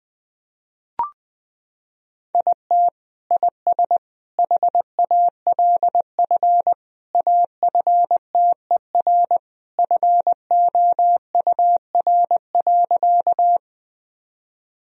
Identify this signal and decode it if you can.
Morse code, heard as a keyed tone, about 20 wpm, 700 Hz. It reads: IT IS HALF AFTER FOUR.
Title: Sentences from Top 1000 Words - Review Part 1 - 20wpm